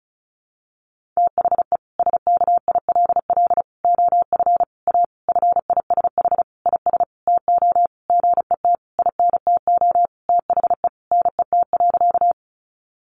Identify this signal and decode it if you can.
Morse code, heard as a keyed tone, about 34 wpm, 700 Hz. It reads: THE SKILL OF A FISH IS TO GET INTO THE NET.
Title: Sentences from Top 1400 Words 35wpm